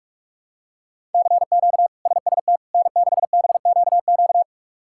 Morse code, heard as a keyed tone, about 41 wpm, 700 Hz. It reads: CQ SST N6BQQ
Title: K1USN SST QSO Practice - 45wpm